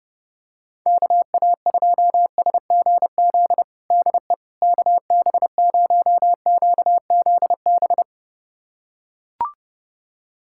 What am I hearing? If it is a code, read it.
Morse code, heard as a keyed tone, about 30 wpm, 700 Hz. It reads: KA2SGZ DE KB0QZB
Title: USA Callsign DE Callsign 30wpm